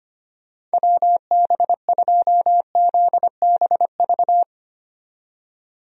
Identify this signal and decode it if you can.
Morse code, heard as a keyed tone, about 25 wpm, 700 Hz. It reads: WB2ZBV
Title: USA Call Signs - 6 Characters - 25wpm